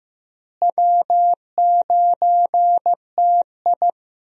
Morse code, heard as a keyed tone, about 15 wpm, 700 Hz. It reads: W9TI
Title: USA Call Signs - 4 Characters - Repeated 3x - 15wpm